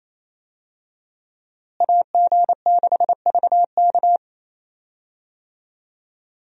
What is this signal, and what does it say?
Morse code, heard as a keyed tone, about 27 wpm, 700 Hz. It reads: AG6VK
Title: USA Call Signs - 5 Characters - Repeated 3x - 28wpm